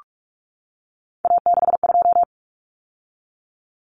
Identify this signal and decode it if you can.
Morse code, heard as a keyed tone, about 44 wpm, 700 Hz. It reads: A62
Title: Common Cut Numbers - 3-Digit 45wpm